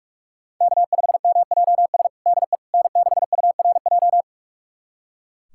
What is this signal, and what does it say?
Morse code, heard as a keyed tone, about 41 wpm, 700 Hz. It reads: K5MJS DE N6URJ
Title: USA Callsign DE Callsign - 5 Characters - 45wpm